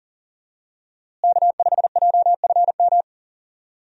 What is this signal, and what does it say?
Morse code, heard as a keyed tone, about 40 wpm, 700 Hz. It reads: K5JFM
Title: USA Call Signs - 5 Characters - Repeated 3x - 40wpm